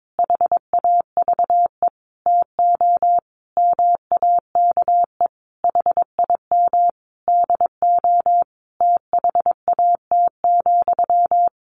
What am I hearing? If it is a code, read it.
Morse code, heard as a keyed tone, about 22 wpm, 700 Hz. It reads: HAVE TO MAKE HIM DO THAT,
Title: A Sound Of Thunder 22wpm